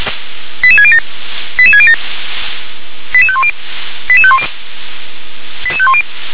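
Five-tone selective call (ZVEI), heard as quick runs of five tones, tones of about 70 ms. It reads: 88607 88607 88510 88510 88510
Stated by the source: ZVEI five-tone selective call received with an SDR.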